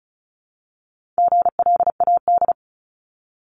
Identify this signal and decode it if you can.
Morse code, heard as a keyed tone, about 34 wpm, 700 Hz. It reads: GLAD
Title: Multi-letter Phonemes Repeated 3x 35wpm